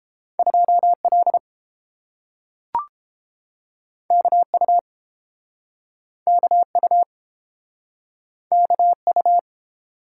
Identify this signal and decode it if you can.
Morse code, heard as a keyed tone, about 30 wpm, 700 Hz. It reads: 2L KU KU KU